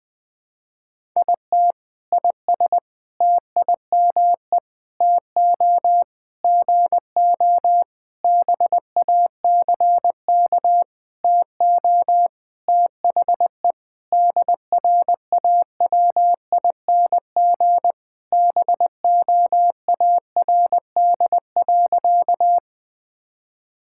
Morse code, heard as a keyed tone, about 20 wpm, 700 Hz. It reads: IT IS TIME TO GO BACK TO THE DRAWING BOARD.
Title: Sentences with 7-Word Phrases - Spoken Beginning - 20wpm